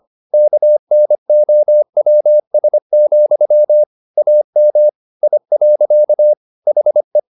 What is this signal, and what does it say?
Morse code, heard as a keyed tone, about 25 wpm, 600 Hz. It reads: KNOWS, AM I. HE
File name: the old man and the sea part 1 25wpm